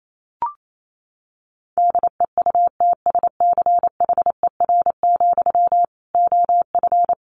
Morse code, heard as a keyed tone, about 28 wpm, 700 Hz. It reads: DEUTSCHER, OF
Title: A Sound Of Thunder 28wpm